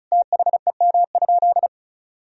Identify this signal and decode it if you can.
Morse code, heard as a keyed tone, about 36 wpm, 700 Hz. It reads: THEM?
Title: Sentences from Top 1000 Words - Review Part 1 - 35wpm